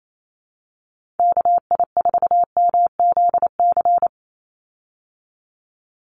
Morse code, heard as a keyed tone, about 28 wpm, 700 Hz. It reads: KI4MZC
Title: Call Signs Repeated 2x - 28wpm